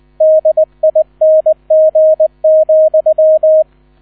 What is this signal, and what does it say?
Morse code, heard as a keyed tone, about 20 wpm, 630 Hz, with radio noise. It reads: DING,
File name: 000 Vorwort 20wpm